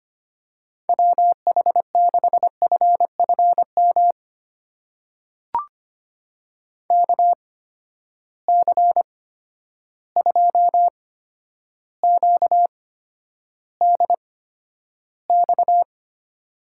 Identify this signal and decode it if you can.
Morse code, heard as a keyed tone, about 25 wpm, spaced as for 11 wpm, 700 Hz. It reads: WH6FFM KC2QDX
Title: USA Call Signs - 6 Characters - 8x Character Spacing - 25wpm